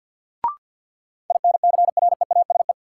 Morse code, heard as a keyed tone, about 53 wpm, 700 Hz. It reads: INCREASE
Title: Sets of 5 Words - New Word Drill - 50wpm